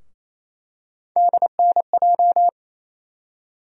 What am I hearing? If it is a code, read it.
Morse code, heard as a keyed tone, about 28 wpm, 700 Hz. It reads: DNJ
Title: USA Call Sign Postfixes - 3 Letters 28wpm